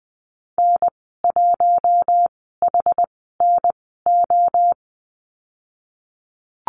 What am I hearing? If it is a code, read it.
Morse code, heard as a keyed tone, about 20 wpm, spaced as for 15 wpm, 700 Hz. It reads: N1HNO
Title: 088 Lesson 15g - Number 5 - Call Signs 20wpm